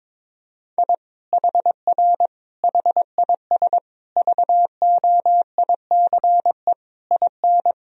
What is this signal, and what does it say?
Morse code, heard as a keyed tone, about 22 wpm, 700 Hz. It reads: I HR HIS VOICE IN
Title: Sentences from Top 500 Words and Abbreviations 22wpm